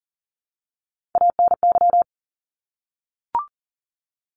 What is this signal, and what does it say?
Morse code, heard as a keyed tone, about 40 wpm, 700 Hz. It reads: ANY
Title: Top 200 Words 40wpm